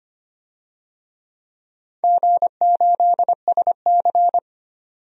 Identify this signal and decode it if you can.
Morse code, heard as a keyed tone, about 25 wpm, 700 Hz. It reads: G8SC
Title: DXCC-100 Call Signs - 4 Characters - 25wpm